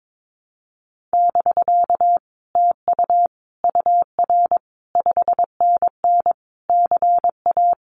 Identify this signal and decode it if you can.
Morse code, heard as a keyed tone, about 22 wpm, 700 Hz. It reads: <BK> TU UR 5NN CA
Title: Common QSO Phrases Encore 22wpm